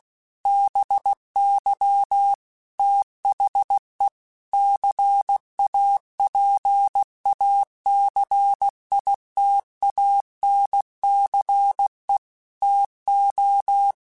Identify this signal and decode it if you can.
Morse code, heard as a keyed tone, about 16 wpm, 800 Hz. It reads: BY THE CAPACITANCE TO